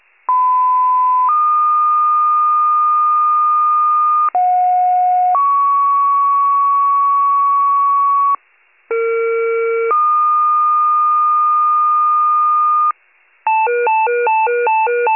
Two-tone page, heard as a one-second tone, then a three-second tone, about 1010 then 1220 Hz, about 730 then 1120 Hz, about 460 then 1220 Hz. It is heard from 0.3 to 12.9 s.